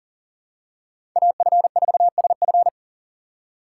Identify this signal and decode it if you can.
Morse code, heard as a keyed tone, about 37 wpm, 700 Hz. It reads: AF4SF
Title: USA Call Signs - 5 Characters - 4x Character Spacing - 40wpm